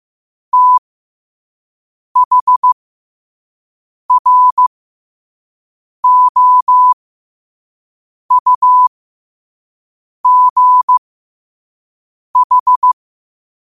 Morse code, heard as a keyed tone, about 15 wpm, spaced as for 5 wpm, 1000 Hz. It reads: THROUGH
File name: qotdfiles / 5wpm